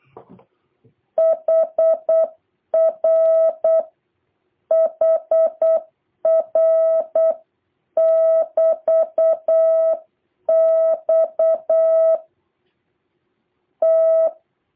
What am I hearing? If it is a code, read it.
Morse code, heard as a keyed tone, about 8 wpm, 650 Hz. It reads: HR HR <BT> X T
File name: jh8cbh / 2023cw